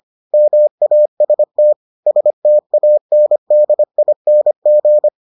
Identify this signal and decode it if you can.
Morse code, heard as a keyed tone, about 25 wpm, 600 Hz. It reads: MAST STANDING